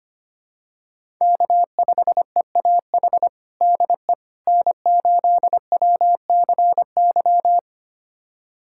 Morse code, heard as a keyed tone, about 25 wpm, 700 Hz. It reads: K5EAH DE N8WCY
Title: USA Callsign DE Callsign - 5 Characters - 25wpm